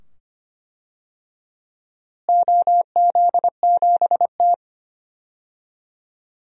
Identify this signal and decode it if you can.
Morse code, heard as a keyed tone, about 25 wpm, 700 Hz. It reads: OZ7T
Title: DXCC-100 Call Signs - 4 Characters - Repeated 2x - 25wpm